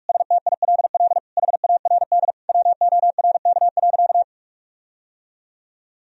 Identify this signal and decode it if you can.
Morse code, heard as a keyed tone, about 46 wpm, 700 Hz. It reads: STILL HARD WORK.
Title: Sentences from Top 1000 Words - Review Part 3 - 45wpm